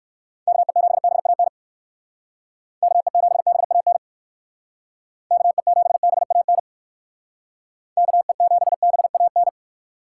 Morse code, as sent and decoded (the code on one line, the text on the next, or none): -.- . --... -... .- -. / -.- . --... -... .- -. / -.- . --... -... .- -. / -.- . --... -... .- -.
KE7BAN KE7BAN KE7BAN KE7BAN